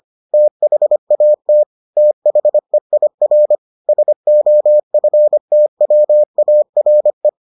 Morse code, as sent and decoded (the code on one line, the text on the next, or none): - .... .- - / - .... .. .-. / ... --- ..-. - .-- .- .-. .
THAT THEIR SOFTWARE